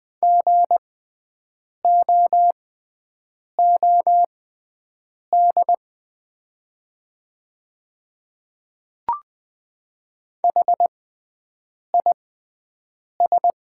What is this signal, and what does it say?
Morse code, heard as a keyed tone, about 20 wpm, spaced as for 7 wpm, 700 Hz. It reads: GOOD HIS
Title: 6x Character Spacing - Top 200 Words - 20wpm